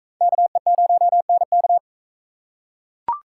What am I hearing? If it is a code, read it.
Morse code, heard as a keyed tone, about 44 wpm, 700 Hz. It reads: KE0NK